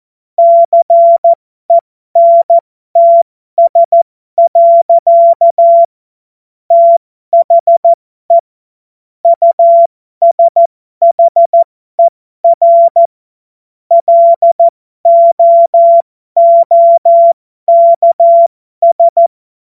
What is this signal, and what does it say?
Morse code, heard as a keyed tone, about 14 wpm, spaced as for 12 wpm, 680 Hz. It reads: CENTS. THE USHER LOOKS